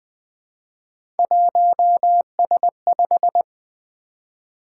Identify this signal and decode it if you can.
Morse code, heard as a keyed tone, about 20 wpm, 700 Hz. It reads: 1S5
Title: Three Letter-Number Combinations - 20wpm